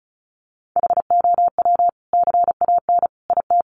Morse code, heard as a keyed tone, about 34 wpm, 700 Hz. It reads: HOW CAN IT
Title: Sentences from Top 1300 Words 35wpm